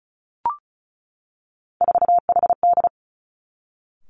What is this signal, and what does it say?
Morse code, heard as a keyed tone, about 36 wpm, 700 Hz. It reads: 4HD
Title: Three Challenging Letter-Number Combinations 35wpm